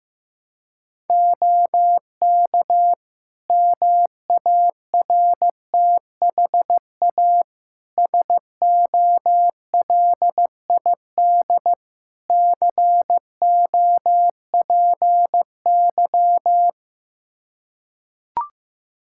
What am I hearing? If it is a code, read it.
Morse code, heard as a keyed tone, about 15 wpm, 700 Hz. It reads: OK MARTHA SOLID COPY
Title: Practice QSO - Exchange 3 - More Info - 15wpm